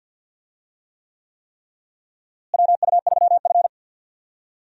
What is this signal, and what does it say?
Morse code, heard as a keyed tone, about 52 wpm, 700 Hz. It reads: WU3F
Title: USA Call Signs - 4 Characters - Repeated 3x - 50wpm